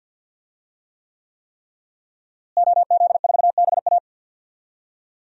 Morse code, heard as a keyed tone, about 49 wpm, 700 Hz. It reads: OZ4BA
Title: DXCC-100 Call Signs - 5 Characters - Repeated 3x - 50wpm